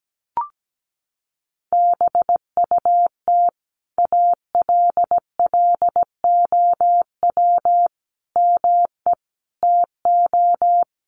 Morse code, heard as a keyed tone, about 17 wpm, 700 Hz. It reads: BUT ALLOW ME TO
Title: Sentences from Top 1100 Words 17wpm